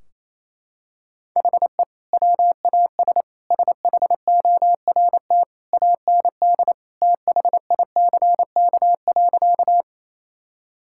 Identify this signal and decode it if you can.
Morse code, heard as a keyed tone, about 27 wpm, 700 Hz. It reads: HE WAS SHORT AND THICK.
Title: Sentences from Top 1600 Words 28wpm